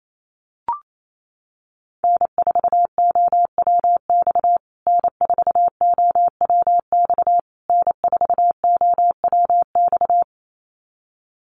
Morse code, heard as a keyed tone, about 28 wpm, 700 Hz. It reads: N4OWX N4OWX N4OWX
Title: USA Call Signs Repeated 3x 28wpm